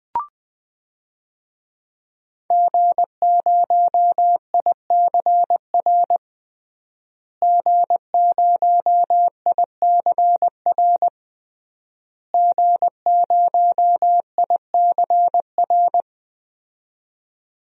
Morse code, heard as a keyed tone, about 20 wpm, 700 Hz. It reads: G0ICR G0ICR G0ICR
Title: DXCC-100 Call Signs - 5 Characters - Repeated 3x - 20wpm